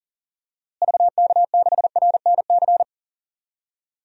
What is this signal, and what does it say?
Morse code, heard as a keyed tone, about 40 wpm, 700 Hz. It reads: VK6RNC
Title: DXCC-100 Call Signs - 6 Characters - Repeated 3x - 40wpm